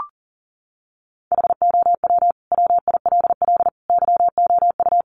Morse code, heard as a keyed tone, about 37 wpm, 700 Hz. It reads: HOW WILL YOU
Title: Sentences from Top 900 Words 40wpm